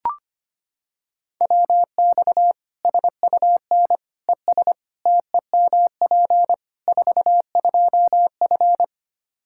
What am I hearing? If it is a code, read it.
Morse code, heard as a keyed tone, about 25 wpm, 700 Hz. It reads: WX SUN ES TEMP 42F
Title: K1USN SST QSO Practice 25wpm